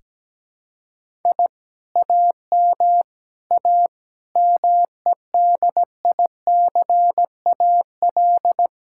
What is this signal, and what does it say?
Morse code, heard as a keyed tone, about 17 wpm, 700 Hz. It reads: I AM A MEDICAL
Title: Sentences from Top 1000 Words - Review Part 5 - 17wpm